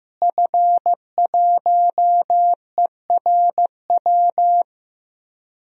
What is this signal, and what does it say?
Morse code, heard as a keyed tone, about 15 wpm, 700 Hz. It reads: F1ERW
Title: DXCC-100 Call Signs - 5 Characters - Repeated 2x - 15wpm